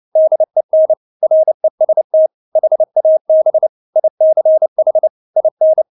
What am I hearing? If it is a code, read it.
Morse code, heard as a keyed tone, about 29 wpm, 630 Hz. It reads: DEN REST HAB ICH IN
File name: Witz10 29wpm